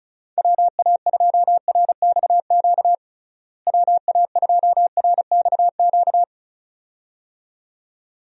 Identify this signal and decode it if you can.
Morse code, heard as a keyed tone, about 36 wpm, 700 Hz. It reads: WA2RXQ WA2RXQ
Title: USA Call Signs - 6 Characters - Repeated 2x - 35wpm